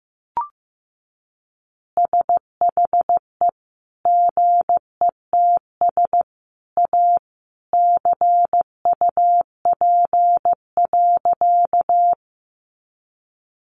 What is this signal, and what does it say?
Morse code, heard as a keyed tone, about 15 wpm, 700 Hz. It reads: SHE GETS A CUP.
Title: Additional Easy Phrases 15wpm